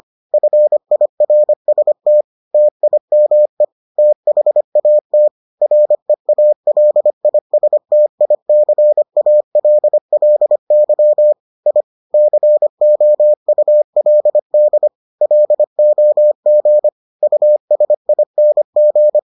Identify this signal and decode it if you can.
Morse code, heard as a keyed tone, about 25 wpm, 600 Hz. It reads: FIRST TIME THAT REALISTICALLY I COULD LOG USING